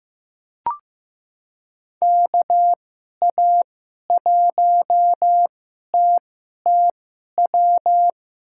Morse code, heard as a keyed tone, about 15 wpm, spaced as for 11 wpm, 700 Hz. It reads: KA1TTW
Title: USA Call Signs - 6 Characters - 2x Character Spacing - 15wpm